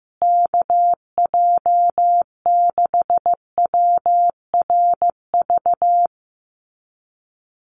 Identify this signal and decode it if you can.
Morse code, heard as a keyed tone, about 15 wpm, 700 Hz. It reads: KJ6WRV
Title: USA Call Signs Repeated 2x 15wpm